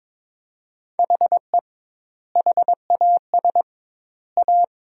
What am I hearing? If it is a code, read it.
Morse code, heard as a keyed tone, about 22 wpm, 700 Hz. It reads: HE HAS A
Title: 2x Word-Spacing - Sentences from Top 500 Words - 22wpm